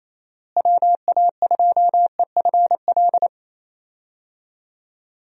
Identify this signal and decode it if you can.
Morse code, heard as a keyed tone, about 28 wpm, 700 Hz. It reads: WA2EFL
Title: USA Call Signs - 6 Characters - Repeated 3x - 28wpm